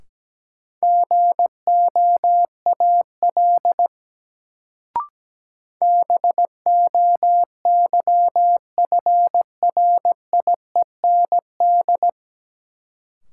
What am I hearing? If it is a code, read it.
Morse code, heard as a keyed tone, about 17 wpm, 700 Hz. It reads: GOAL BOYFRIEND
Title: Sentences from Top 1000 Words - Review Part 5 - New Word Drill - 17wpm